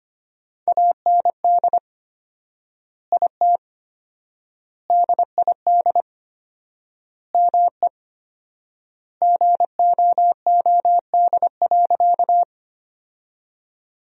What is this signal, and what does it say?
Morse code, heard as a keyed tone, about 25 wpm, 700 Hz. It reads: AND IT DID ME GOOD.
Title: 4x Word-Spacing - Sentences from Top 200 Words - 25wpm